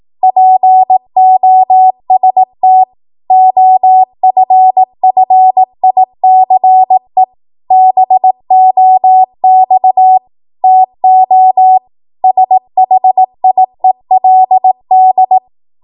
Morse code, heard as a keyed tone, about 18 wpm, 750 Hz. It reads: POST OFFICE BOX TO SHIELD
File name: Morse / 18WPM